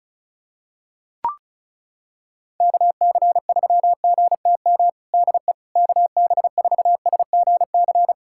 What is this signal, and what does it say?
Morse code, heard as a keyed tone, about 36 wpm, 690 Hz. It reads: KC3GTM DE KB4SGC
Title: USA Callsign DE Callsign - 6 Characters - 35wpm